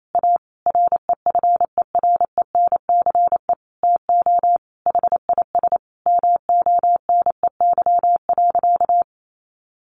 Morse code, heard as a keyed tone, about 28 wpm, 700 Hz. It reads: A REFERENCE TO HIS MONEY.
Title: Sentences from Top 1200 Words 28wpm